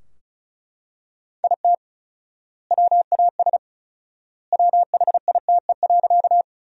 Morse code, heard as a keyed tone, about 36 wpm, 700 Hz. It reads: IT WAS WHITE.
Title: 4x Word-Spacing - Sentences from Top 300 Words - 35wpm